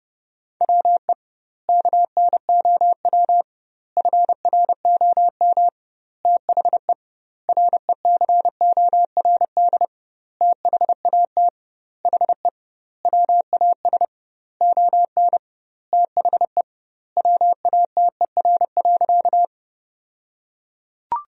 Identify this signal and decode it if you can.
Morse code, heard as a keyed tone, about 30 wpm, 700 Hz. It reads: WE KNOW FROM THE RECORD THAT HE WAS ON THE WATER.
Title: 2x Word-Spacing - Sentences from Top 500 Words - 30wpm